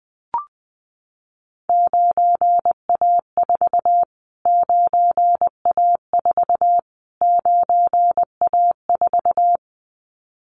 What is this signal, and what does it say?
Morse code, heard as a keyed tone, about 20 wpm, 700 Hz. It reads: 9A4 9A4 9A4
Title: Callsign Prefixes for Top 25 DXCC - Repeated 3x - 20wpm